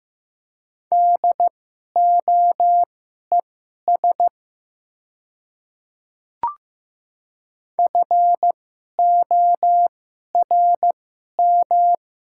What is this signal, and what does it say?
Morse code, heard as a keyed tone, about 15 wpm, spaced as for 11 wpm, 700 Hz. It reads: DOES FORM
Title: 2x Character Spacing - Top 200 Words - 15wpm